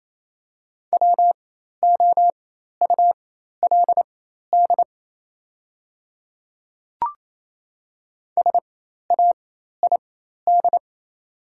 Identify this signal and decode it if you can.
Morse code, heard as a keyed tone, about 27 wpm, spaced as for 13 wpm, 690 Hz. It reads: WOULD SAID
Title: 4x Character Spacing - Top 100 Words - 28wpm